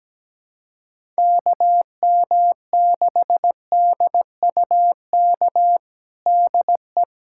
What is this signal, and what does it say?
Morse code, heard as a keyed tone, about 17 wpm, 700 Hz. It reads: KM6DUK DE
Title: USA Callsign DE Callsign - 6 Characters - 17wpm